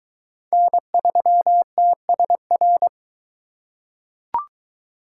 Morse code, heard as a keyed tone, about 23 wpm, 700 Hz. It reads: N3TSR